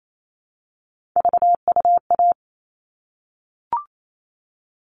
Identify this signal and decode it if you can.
Morse code, heard as a keyed tone, about 28 wpm, 700 Hz. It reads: VUA
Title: All Cut Numbers - 3-Digit 28wpm